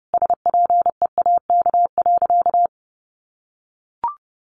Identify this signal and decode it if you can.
Morse code, heard as a keyed tone, about 30 wpm, 700 Hz. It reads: SPEAK.
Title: Sentences from Top 1100 Words 30wpm